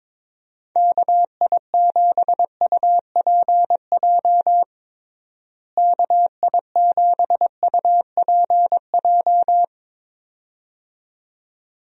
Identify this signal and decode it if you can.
Morse code, heard as a keyed tone, about 22 wpm, 700 Hz. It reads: KI7UPJ KI7UPJ
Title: USA Call Signs - 6 Characters - Repeated 2x - 22wpm